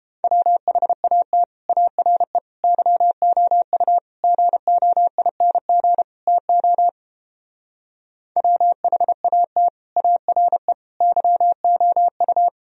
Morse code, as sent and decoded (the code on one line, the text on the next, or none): .-- .... .- - / .- .-. / -.-- --- ..- / --. --- .. -. --. / - --- / .-- .... .- - / .- .-. / -.-- --- ..-
WHAT ARE YOU GOING TO WHAT ARE YOU